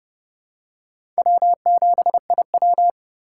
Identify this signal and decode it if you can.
Morse code, heard as a keyed tone, about 29 wpm, 700 Hz. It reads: W7IW
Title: USA Call Signs - 4 Characters - 2x Character Spacing - 30wpm